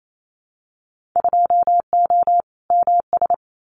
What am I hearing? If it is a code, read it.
Morse code, heard as a keyed tone, about 28 wpm, 700 Hz. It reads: 2O MS
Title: Winter Field Day Exchanges 28wpm